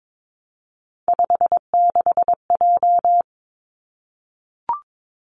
Morse code, heard as a keyed tone, about 22 wpm, 700 Hz. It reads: 56J
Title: Three Challenging Letter-Number Combinations 22wpm